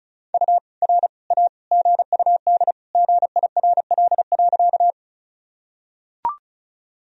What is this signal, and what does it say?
Morse code, heard as a keyed tone, about 34 wpm, 700 Hz. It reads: U R A GUD GIRL.
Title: Sentences from Top 400 Words and Abbreviations 35wpm